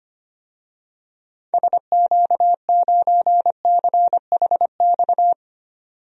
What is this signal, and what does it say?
Morse code, heard as a keyed tone, about 25 wpm, 700 Hz. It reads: SQ9CHX
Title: DXCC-100 Call Signs - 6 Characters - 25wpm